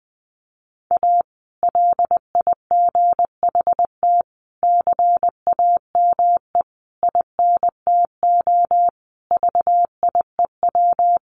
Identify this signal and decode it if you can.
Morse code, heard as a keyed tone, about 20 wpm, 700 Hz. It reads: A LIGHT CAME INTO VIEW
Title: Sentences from Top 1100 Words 20wpm